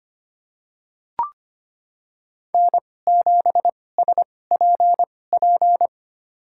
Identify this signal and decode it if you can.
Morse code, heard as a keyed tone, about 25 wpm, spaced as for 18 wpm, 700 Hz. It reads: N7SPP
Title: USA Call Signs - 2x Character Spacing - 25wpm